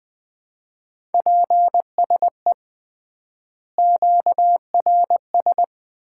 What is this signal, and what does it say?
Morse code, heard as a keyed tone, about 20 wpm, 700 Hz. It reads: PSE QRS
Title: Common QSO Phrases - 3x Word-Spacing - 20wpm